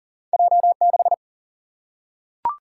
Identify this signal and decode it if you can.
Morse code, heard as a keyed tone, about 42 wpm, 700 Hz. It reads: J 6
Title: Single-Letter Single-Number 40wpm